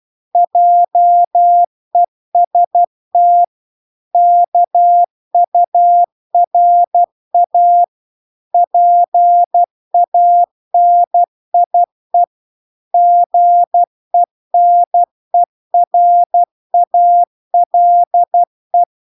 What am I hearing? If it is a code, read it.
Morse code, heard as a keyed tone, about 12 wpm, 700 Hz. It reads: JEST KURA PANIE GENERALE